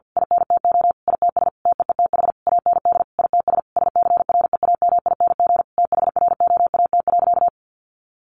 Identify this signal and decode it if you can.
Morse code, heard as a keyed tone, about 50 wpm, 700 Hz. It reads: INTO ITS TEETH AND ITS SCREAMING THROAT.
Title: A Sound Of Thunder 50wpm